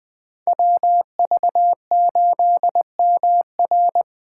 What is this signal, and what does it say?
Morse code, heard as a keyed tone, about 20 wpm, 700 Hz. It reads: WV8MR